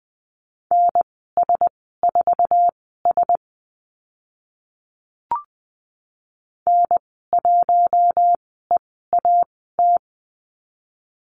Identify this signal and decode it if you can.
Morse code, heard as a keyed tone, about 20 wpm, spaced as for 15 wpm, 700 Hz. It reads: NS4S N1EAT
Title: 069 Lesson 12g - Letter d - Call Signs 20wpm